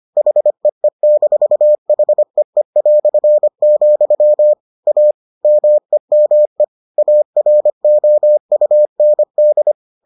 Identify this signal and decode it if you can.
Morse code, heard as a keyed tone, about 25 wpm, 600 Hz. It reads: HEE-HEE", A MEME AROUND